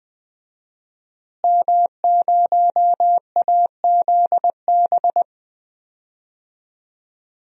Morse code, heard as a keyed tone, about 20 wpm, 700 Hz. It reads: M0AZB
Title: DXCC-100 Call Signs - 5 Characters - Repeated 2x - 20wpm